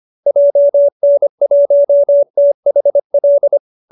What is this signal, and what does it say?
Morse code, heard as a keyed tone, about 25 wpm, 570 Hz. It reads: JN1THL